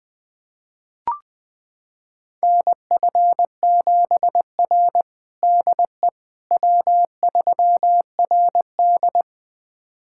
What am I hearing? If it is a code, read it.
Morse code, heard as a keyed tone, about 21 wpm, 700 Hz. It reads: NF7R DE W3RD
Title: USA Callsign DE Callsign - 4 Characters - 20wpm